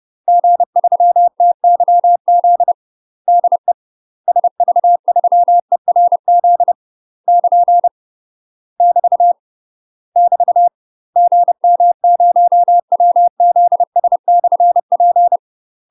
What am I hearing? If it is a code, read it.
Morse code, heard as a keyed tone, about 30 wpm, 700 Hz. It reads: G3TYZ DE SV3ERZ <KN> <BT> <BT> GM0WZS/P
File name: UK MASTER QSOs TRAINING / 30WPM / UK MASTER QSOs 0047